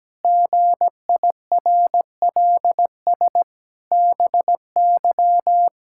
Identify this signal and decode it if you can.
Morse code, heard as a keyed tone, about 17 wpm, 700 Hz. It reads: GIRLS BY
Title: Pangram Challenge v2 - 17wpm